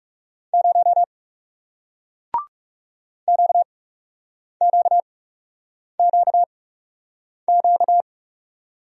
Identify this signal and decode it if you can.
Morse code, heard as a keyed tone, about 38 wpm, spaced as for 8 wpm, 700 Hz. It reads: 0 QQQQ